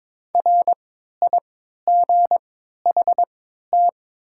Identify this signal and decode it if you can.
Morse code, heard as a keyed tone, about 23 wpm, spaced as for 13 wpm, 700 Hz. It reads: RIGHT
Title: 3x Character Spacing - Top 200 Words - 22wpm